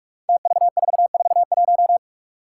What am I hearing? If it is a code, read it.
Morse code, heard as a keyed tone, about 46 wpm, 700 Hz. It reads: TV441
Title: Five Challenging Letter-Number Combinations 45wpm